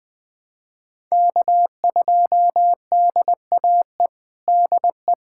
Morse code, heard as a keyed tone, about 20 wpm, 700 Hz. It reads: K2DAE DE
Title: USA Callsign DE Callsign 20wpm